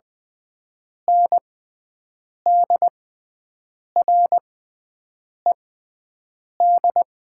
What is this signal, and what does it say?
Morse code, heard as a keyed tone, about 20 wpm, spaced as for 7 wpm, 700 Hz. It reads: NDRED
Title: 6x Character Spacing - Top 500 Words - 20wpm